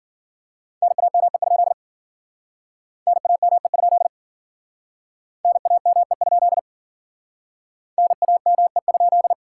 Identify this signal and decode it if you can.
Morse code, heard as a keyed tone, about 50 wpm, 700 Hz. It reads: NAME? NAME? NAME? NAME?